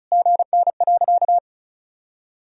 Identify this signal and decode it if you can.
Morse code, heard as a keyed tone, about 34 wpm, 700 Hz. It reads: GN.
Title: Sentences from Top 300 Words and Abbreviations 35wpm